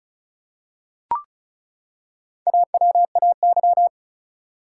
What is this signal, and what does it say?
Morse code, heard as a keyed tone, about 36 wpm, 700 Hz. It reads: AWAY
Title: Four Letter Words - 35wpm